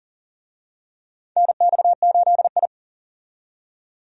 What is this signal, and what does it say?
Morse code, heard as a keyed tone, about 40 wpm, 700 Hz. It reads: NX8I
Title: USA Call Signs - 4 Characters - 40wpm